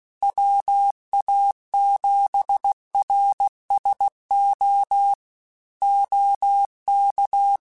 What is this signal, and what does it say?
Morse code, heard as a keyed tone, about 16 wpm, 800 Hz. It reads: WA7RSO OK
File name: QSOs / QSOs-mp3s / WD0FBT 1@15